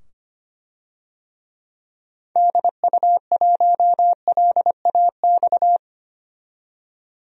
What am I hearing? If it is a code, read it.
Morse code, heard as a keyed tone, about 25 wpm, 700 Hz. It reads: DU1LAX
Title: DXCC-100 Call Signs - 6 Characters - 25wpm